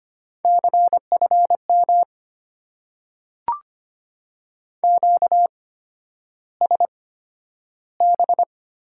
Morse code, heard as a keyed tone, about 25 wpm, spaced as for 11 wpm, 690 Hz. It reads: CFM QSB
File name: Top 90 QSO Elements - 8x Character Spacing - 25wpm